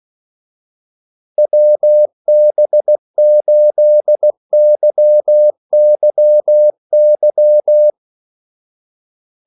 Morse code, heard as a keyed tone, about 16 wpm, 600 Hz. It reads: WB8YYY
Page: Kaldesignaler 16 wpm